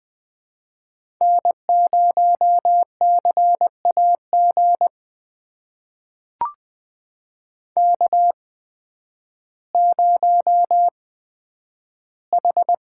Morse code, heard as a keyed tone, about 20 wpm, spaced as for 11 wpm, 700 Hz. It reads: N0CAG K0H
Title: USA Call Signs - 5 Characters - 8x Character Spacing - 20wpm